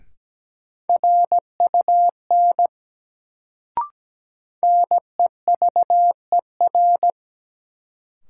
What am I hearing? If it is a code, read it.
Morse code, heard as a keyed tone, about 17 wpm, 700 Hz. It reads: RUN NEVER
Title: Sentences from Top 300 Words - New Word Drill - 17wpm